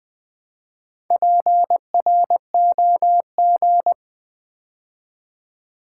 Morse code, heard as a keyed tone, about 20 wpm, 700 Hz. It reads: PROG
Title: Build A Word - 2 Letters At A Time - 8-Letter Words - 20wpm